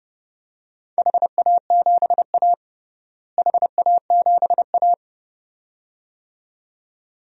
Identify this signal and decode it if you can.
Morse code, heard as a keyed tone, about 32 wpm, 700 Hz. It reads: HA7A HA7A
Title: DXCC-100 Call Signs - 4 Characters - Repeated 2x - 30wpm